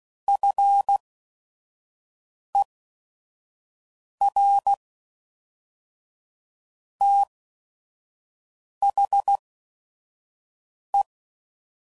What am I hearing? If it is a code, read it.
Morse code, heard as a keyed tone, about 16 wpm, spaced as for 5 wpm, 800 Hz. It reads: FER THE
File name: CW Code Groups @5WPM / QSOs / kc7sac